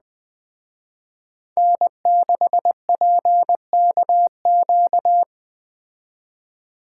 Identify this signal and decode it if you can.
Morse code, heard as a keyed tone, about 20 wpm, 700 Hz. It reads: N6PKQ
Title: Speed Racing - USA 1x3 Call Signs 17wpm